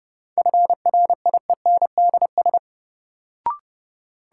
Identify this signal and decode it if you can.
Morse code, heard as a keyed tone, about 30 wpm, 700 Hz. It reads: FRIENDS
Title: Sentences from Top 1000 Words - Review Part 5 - New Word Drill - 30wpm